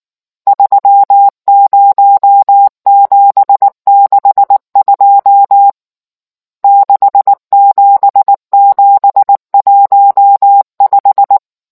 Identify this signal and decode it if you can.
Morse code, heard as a keyed tone, about 19 wpm, 800 Hz. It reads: 30762 67715